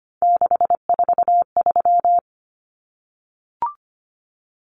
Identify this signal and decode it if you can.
Morse code, heard as a keyed tone, about 25 wpm, 700 Hz. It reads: 643
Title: Numbers - 3-Digit 25wpm